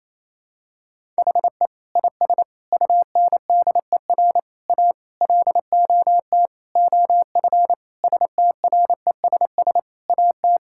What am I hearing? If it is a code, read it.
Morse code, heard as a keyed tone, about 28 wpm, 700 Hz. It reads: HE IS UNDER A LOT OF STRESS AT
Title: Sentences from Top 1000 Words - Review Part 6 - 28wpm